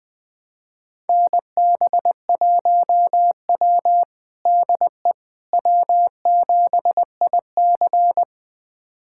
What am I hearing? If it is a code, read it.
Morse code, heard as a keyed tone, about 20 wpm, 700 Hz. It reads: NB1W DE W7IC
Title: USA Callsign DE Callsign - 4 Characters - 20wpm